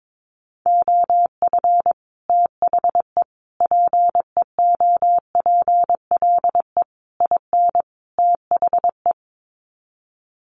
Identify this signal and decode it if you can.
Morse code, heard as a keyed tone, about 22 wpm, 700 Hz. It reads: OF THE PEOPLE IN THE
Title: Sets of 5 Words Encore 22wpm